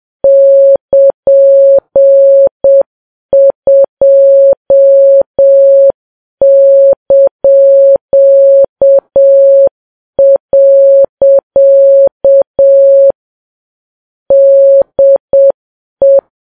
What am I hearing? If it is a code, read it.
Morse code, heard as a keyed tone, about 7 wpm, 550 Hz. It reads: <KN>2). DE